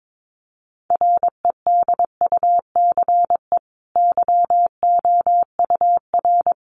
Morse code, heard as a keyed tone, about 22 wpm, 700 Hz. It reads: REDUCE YOUR
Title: Sentences from Top 1100 Words 22wpm